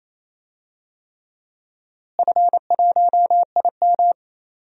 Morse code, heard as a keyed tone, about 27 wpm, 700 Hz. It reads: F1IM
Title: DXCC-100 Call Signs - 4 Characters - Repeated 3x - 28wpm